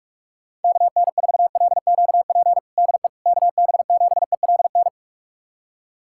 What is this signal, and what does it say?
Morse code, heard as a keyed tone, about 46 wpm, 700 Hz. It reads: KN4LQP DE KB7ELN
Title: USA Callsign DE Callsign 45wpm